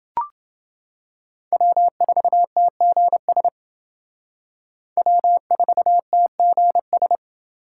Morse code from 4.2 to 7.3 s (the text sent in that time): W4TGS